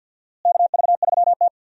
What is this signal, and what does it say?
Morse code, heard as a keyed tone, about 49 wpm, 700 Hz. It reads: KV3T
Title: USA Call Signs - 4 Characters - Repeated 2x - 50wpm